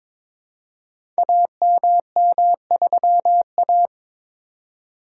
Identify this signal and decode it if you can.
Morse code, heard as a keyed tone, about 22 wpm, 700 Hz. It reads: AMM3A
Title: Code Groups - 5 Letter:Numbers - 22wpm